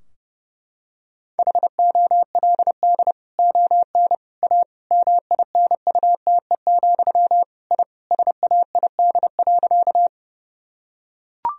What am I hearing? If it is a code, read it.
Morse code, heard as a keyed tone, about 30 wpm, 700 Hz. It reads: HOLD ON A MINUTE, I SAID.
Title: Sentences from Top 1000 Words - Review Part 2 - 30wpm